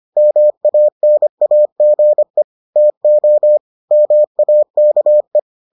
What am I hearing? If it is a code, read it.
Morse code, heard as a keyed tone, about 25 wpm, 600 Hz. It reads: MANAGE TO MAKE